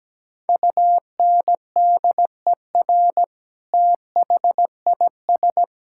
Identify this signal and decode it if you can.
Morse code, heard as a keyed tone, about 17 wpm, 700 Hz. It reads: UNDER THIS
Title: Sentences from Top 1000 Words - Review Part 5 - 17wpm